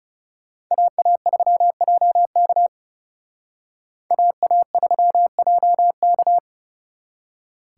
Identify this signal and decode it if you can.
Morse code, heard as a keyed tone, about 34 wpm, 700 Hz. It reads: AA3JK AA3JK
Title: Speed Racing - USA 2x2 Call Signs 30wpm